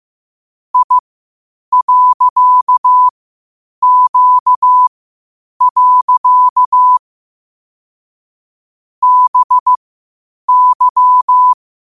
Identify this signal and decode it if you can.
Morse code, heard as a keyed tone, about 15 wpm, spaced as for 8 wpm, 1000 Hz. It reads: I.Q. BY